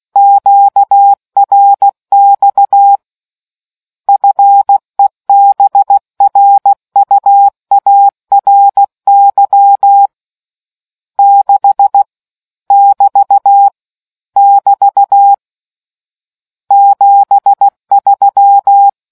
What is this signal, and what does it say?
Morse code, heard as a keyed tone, about 16 wpm, 800 Hz. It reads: QRX FEBRUARY 6 <BT> <BT> 73